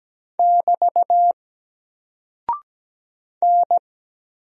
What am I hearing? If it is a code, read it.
Morse code, heard as a keyed tone, about 17 wpm, 700 Hz. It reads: <BT> N
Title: Single Character (Letter:Number:Punctuation:Prosign) 17wpm